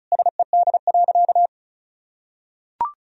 Morse code, heard as a keyed tone, about 34 wpm, 700 Hz. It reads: SED.
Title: Sentences from Top 100 Words and Abbreviations 35wpm